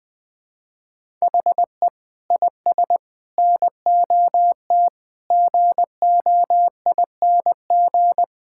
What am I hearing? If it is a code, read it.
Morse code, heard as a keyed tone, about 20 wpm, 700 Hz. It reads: HE IS NOT GOING
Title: Sentences with 5-Word Phrases - Spoken Beginning - 20wpm